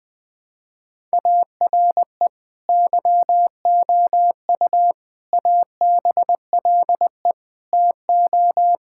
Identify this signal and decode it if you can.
Morse code, heard as a keyed tone, about 21 wpm, 700 Hz. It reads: ARE YOU ABLE TO
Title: Sentences from Top 1000 Words - Review Part 6 - 20wpm